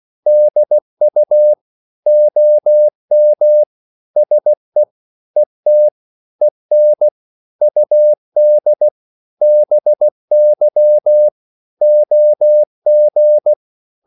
Morse code, heard as a keyed tone, about 16 wpm, 600 Hz. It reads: DU OM SE ET EN UD BY OG
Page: Korte danske ord 16wpm